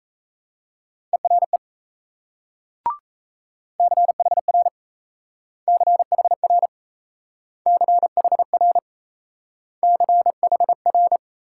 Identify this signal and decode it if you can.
Morse code, heard as a keyed tone, about 37 wpm, 700 Hz. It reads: ERE CHR CHR CHR CHR